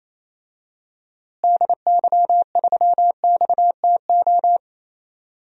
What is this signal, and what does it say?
Morse code, heard as a keyed tone, about 28 wpm, 700 Hz. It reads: DY3XTO
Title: DXCC-100 Call Signs - 6 Characters - Repeated 3x - 28wpm